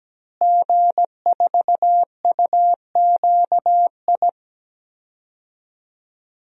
Morse code, heard as a keyed tone, about 17 wpm, 700 Hz. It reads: G4UQI
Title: DXCC-100 Call Signs - 5 Characters - Repeated 2x - 17wpm